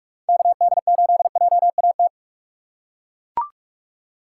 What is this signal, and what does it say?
Morse code, heard as a keyed tone, about 42 wpm, 700 Hz. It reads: KD8JAT
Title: USA Call Signs - 6 Characters - 6x Character Spacing - 45wpm